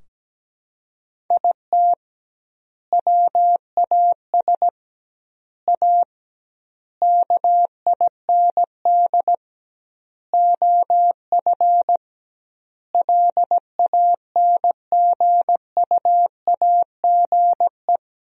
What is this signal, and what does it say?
Morse code, heard as a keyed tone, about 17 wpm, 700 Hz. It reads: IT WAS A KIND OF LANGUAGE
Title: 2x Word-Spacing - Sentences from Top 500 Words - 17wpm